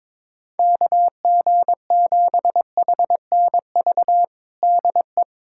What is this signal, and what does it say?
Morse code, heard as a keyed tone, about 22 wpm, 700 Hz. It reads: KG7HNV DE
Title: USA Callsign DE Callsign - 6 Characters - 22wpm